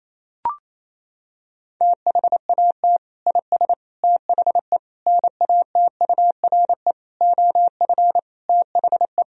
Morse code, heard as a keyed tone, about 27 wpm, 700 Hz. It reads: THAT IS THE NATURE OF THE
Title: Sentences from Top 1000 Words - Review Part 4 - 28wpm